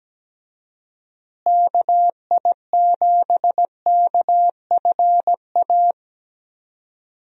Morse code, heard as a keyed tone, about 17 wpm, 700 Hz. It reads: KI7KFA
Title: USA Call Signs - 6 Characters - Repeated 3x - 17wpm